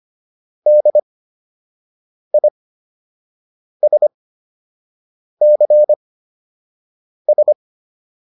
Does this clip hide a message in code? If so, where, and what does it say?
Yes: Morse code from 0.7 to 7.5 s, heard as a keyed tone, about 25 wpm, spaced as for 6 wpm, 600 Hz. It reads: DISCS